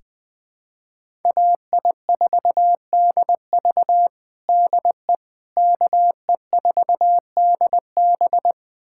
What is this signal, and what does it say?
Morse code, heard as a keyed tone, about 20 wpm, 700 Hz. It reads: AI4DV DE KE4DB
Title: USA Callsign DE Callsign - 5 Characters - 20wpm